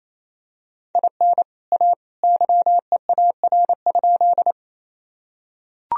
Morse code, heard as a keyed tone, about 27 wpm, 700 Hz. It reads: IN A YEAR?
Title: Sentences from Top 1000 Words - Review Part 1 - 28wpm